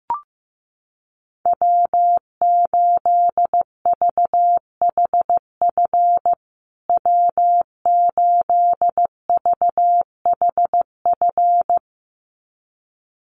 Morse code, heard as a keyed tone, about 14 wpm, 700 Hz. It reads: W8VHF W8VHF
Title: USA Call Signs Repeated 2x 15wpm